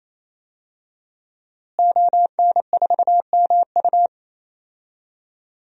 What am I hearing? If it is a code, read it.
Morse code, heard as a keyed tone, about 27 wpm, 700 Hz. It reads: ON4MU
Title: DXCC-100 Call Signs - 5 Characters - Repeated 3x - 28wpm